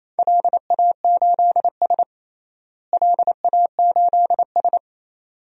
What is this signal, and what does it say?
Morse code, heard as a keyed tone, about 28 wpm, 700 Hz. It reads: LA8S LA8S
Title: DXCC-100 Call Signs - 4 Characters - Repeated 2x - 28wpm